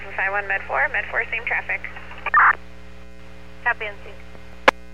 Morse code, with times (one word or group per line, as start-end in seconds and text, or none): none